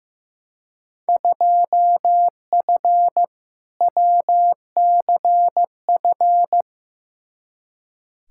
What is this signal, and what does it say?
Morse code, heard as a keyed tone, about 15 wpm, 700 Hz. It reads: 2F WCF
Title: Field Day Exchanges Encore - 15wpm